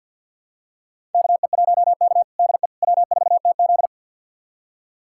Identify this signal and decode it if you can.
Morse code, heard as a keyed tone, about 52 wpm, 700 Hz. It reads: KE1K DE W4TZ
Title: USA Callsign DE Callsign - 4 Characters - 50wpm